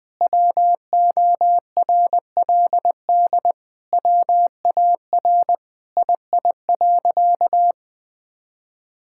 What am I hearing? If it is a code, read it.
Morse code, heard as a keyed tone, about 20 wpm, 700 Hz. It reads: WORLD WAR II.
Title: Sentences with 6-Word Phrases - 20wpm